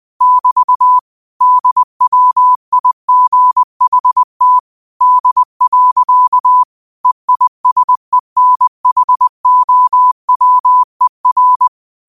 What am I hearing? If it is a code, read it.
Morse code, heard as a keyed tone, about 20 wpm, 1000 Hz. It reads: <BT> DWIGHT D. EISENHOWER